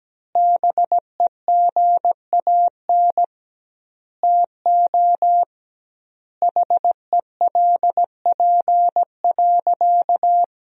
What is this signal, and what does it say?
Morse code, heard as a keyed tone, about 17 wpm, 700 Hz. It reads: BEGAN TO HELP.
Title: 2x Word-Spacing - Sentences from Top 300 Words - 17wpm